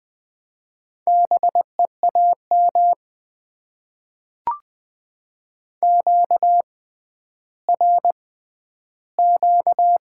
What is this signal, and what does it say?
Morse code, heard as a keyed tone, about 20 wpm, spaced as for 11 wpm, 700 Hz. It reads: BEAM QRQ
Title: Top 90 QSO Elements - 6x Character Spacing - 20wpm